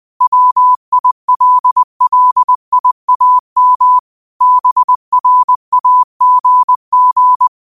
Morse code, heard as a keyed tone, about 20 wpm, 1000 Hz. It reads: WILLIAM BRAGG
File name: qotdfiles / 20wpm